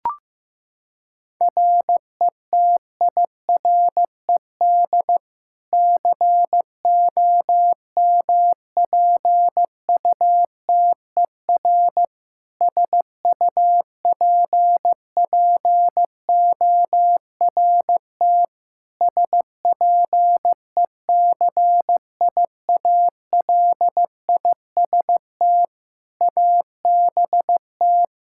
Morse code, heard as a keyed tone, about 15 wpm, 700 Hz. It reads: RETIRED COMPUTER SUPPORT SPECIALIST ABT
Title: Practice QSO - Exchange 3 - More Info - 15wpm